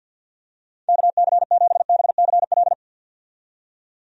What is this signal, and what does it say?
Morse code, heard as a keyed tone, about 50 wpm, 700 Hz. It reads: KC7BCL
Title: USA Call Signs - 6 Characters - 50wpm